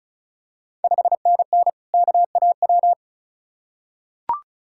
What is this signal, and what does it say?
Morse code, heard as a keyed tone, about 36 wpm, 700 Hz. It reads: 5NN KAW
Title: Ontario QSO Party Exchange 35wpm